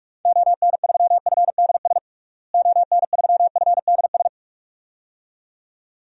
Morse code, heard as a keyed tone, about 41 wpm, 700 Hz. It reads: ON3FDS ON3FDS
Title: DXCC-100 Call Signs - 6 Characters - Repeated 2x - 45wpm